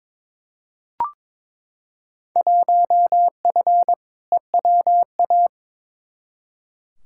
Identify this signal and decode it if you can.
Morse code, heard as a keyed tone, about 22 wpm, 700 Hz. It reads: 1F EWA
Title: Field Day Exchanges Encore - 22wpm